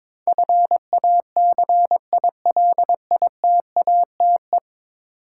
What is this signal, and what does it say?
Morse code, heard as a keyed tone, about 22 wpm, 700 Hz. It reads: FACILITATE
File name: practice / 22wpm / 10 Letter Words - First 3 Letters Given - 22wpm